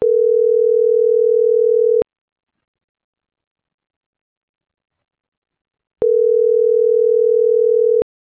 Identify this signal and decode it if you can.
Ringback tone (North America), 440 + 480 Hz.